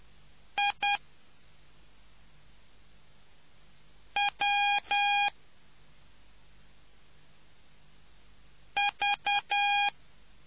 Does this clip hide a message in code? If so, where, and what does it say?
Yes: Morse code from 0.6 to 9.9 s, heard as a keyed tone, about 10 wpm, 810 Hz. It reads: I W V